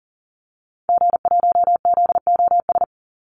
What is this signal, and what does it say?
Morse code, heard as a keyed tone, about 38 wpm, 700 Hz. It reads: G1ZOS
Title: Call Signs - 40wpm